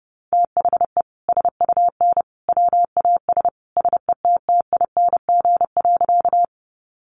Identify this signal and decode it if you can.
Morse code, heard as a keyed tone, about 30 wpm, 700 Hz. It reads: THE SUN WAS SETTING.
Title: Sentences from Top 1300 Words 30wpm